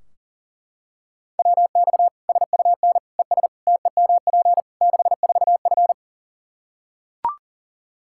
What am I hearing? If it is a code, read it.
Morse code, heard as a keyed tone, about 37 wpm, 700 Hz. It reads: WX SUN ES TEMP 64F
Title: Practice QSO - Exchange 2 - Info - 40wpm